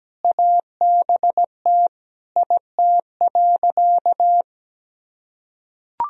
Morse code, heard as a keyed tone, about 17 wpm, 700 Hz. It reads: ABT IT.
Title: Sentences from Top 500 Words and Abbreviations 17wpm